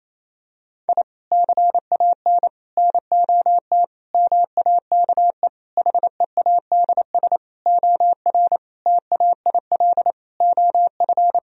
Morse code, heard as a keyed tone, about 28 wpm, 700 Hz. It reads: I CAN NOT MAKE HEADS OR TAIL OF
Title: Sentences from Top 1000 Words - Review Part 3 - 28wpm